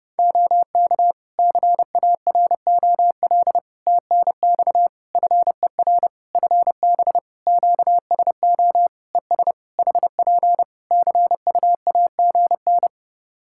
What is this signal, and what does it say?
Morse code, heard as a keyed tone, about 30 wpm, 700 Hz. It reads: OK CAROL TNX FER FB QSO ES HP CUAGN
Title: Practice QSO - Exchange 4 - Ending - 30wpm